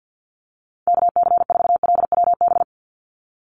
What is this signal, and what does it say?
Morse code, heard as a keyed tone, about 50 wpm, 700 Hz. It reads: KC4LWB
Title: USA Call Signs 50wpm